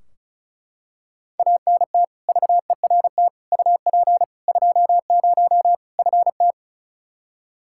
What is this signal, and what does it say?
Morse code, heard as a keyed tone, about 36 wpm, 700 Hz. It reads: ANT VERT UP 20 FT
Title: Practice QSO - Exchange 2 - Info - 35wpm